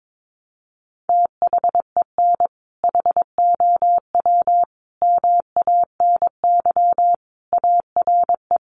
Morse code, heard as a keyed tone, about 22 wpm, 700 Hz. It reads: THEN HOW MANY ARE
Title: Sentences from Top 100 Words 22wpm